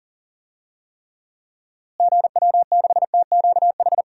Morse code, heard as a keyed tone, about 40 wpm, 700 Hz. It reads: GW6TQH
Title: DXCC-100 Call Signs - 6 Characters - Repeated 2x - 40wpm